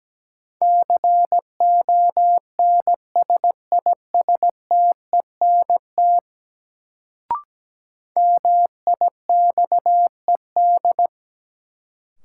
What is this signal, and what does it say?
Morse code, heard as a keyed tone, about 17 wpm, 700 Hz. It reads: CONSISTENT MIXED
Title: Sentences from Top 1500 Words - New Word Drill Alt - 17wpm